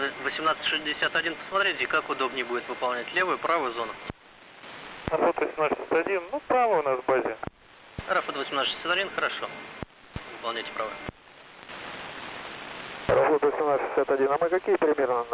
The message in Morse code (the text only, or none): none